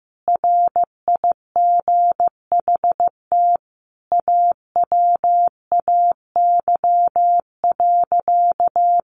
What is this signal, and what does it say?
Morse code, heard as a keyed tone, about 15 wpm, 700 Hz. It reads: RIGHT AWAY.
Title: Sentences from Top 1500 Words 15wpm